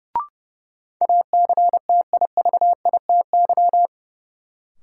Morse code, heard as a keyed tone, about 30 wpm, 700 Hz. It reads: ACTIVITY
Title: Sentences from Top 1000 Words - Review Part 4 - New Word Drill - 30wpm